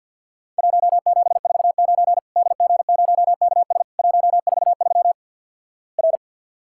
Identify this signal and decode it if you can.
Morse code, heard as a keyed tone, about 50 wpm, 700 Hz. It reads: 1749 DG0KS 143 R
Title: Worked All Europe DX Contest - QTC Exchange - 50wpm